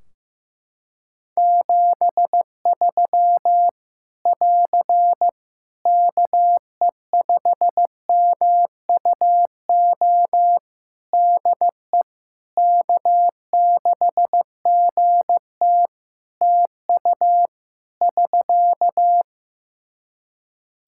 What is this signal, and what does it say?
Morse code, heard as a keyed tone, about 15 wpm, 700 Hz. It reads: 73 <AR> KE5MUO DE K6GT TU <SK>
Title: Practice QSO - Exchange 4 - Ending - 15wpm